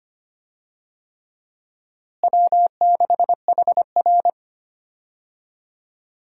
Morse code, heard as a keyed tone, about 25 wpm, 700 Hz. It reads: W6HR
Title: USA Call Signs - 4 Characters - Repeated 3x - 25wpm